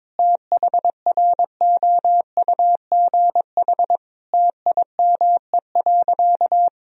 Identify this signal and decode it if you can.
Morse code, heard as a keyed tone, about 21 wpm, 700 Hz. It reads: THROUGH TIME.
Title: Sentences from Top 1000 Words - Review Part 2 - 22wpm